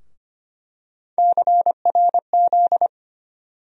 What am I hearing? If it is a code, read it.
Morse code, heard as a keyed tone, about 25 wpm, 700 Hz. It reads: CRZ
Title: USA Call Sign Postfixes - 3 Letters 25wpm